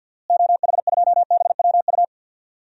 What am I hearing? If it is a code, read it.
Morse code, heard as a keyed tone, about 53 wpm, 700 Hz. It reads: OH2BWU
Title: DXCC-100 Call Signs - 6 Characters - Repeated 2x - 50wpm